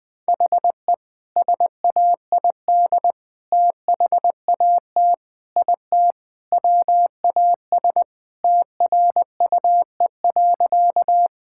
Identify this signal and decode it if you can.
Morse code, heard as a keyed tone, about 21 wpm, 700 Hz. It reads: HE SAID THAT IT WAS TRUE.
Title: Sentences with 2-Word Phrases - 20wpm